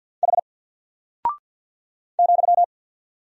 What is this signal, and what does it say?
Morse code, heard as a keyed tone, about 48 wpm, 700 Hz. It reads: H ,
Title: Single Character (Letter:Number:Punctuation:Prosign) Alt 50wpm